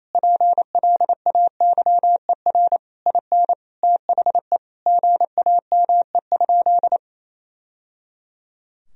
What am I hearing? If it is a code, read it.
Morse code, heard as a keyed tone, about 27 wpm, 700 Hz. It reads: PLAYER IN THE GAME?
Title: Sentences from Top 1000 Words - Review Part 4 - 28wpm